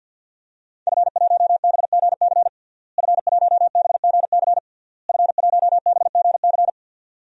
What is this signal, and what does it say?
Morse code, heard as a keyed tone, about 50 wpm, 700 Hz. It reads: F1BGC F1BGC F1BGC
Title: DXCC-100 Call Signs - 5 Characters - Repeated 3x - 50wpm